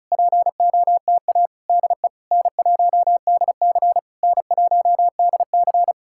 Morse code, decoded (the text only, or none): POTA DE N1DC N1DC